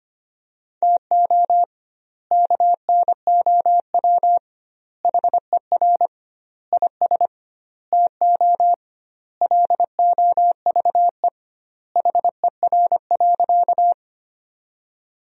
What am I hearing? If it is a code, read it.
Morse code, heard as a keyed tone, about 25 wpm, 700 Hz. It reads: TO KNOW HER IS TO LOVE HER.
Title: 2x Word-Spacing - Sentences from Top 400 Words - 25wpm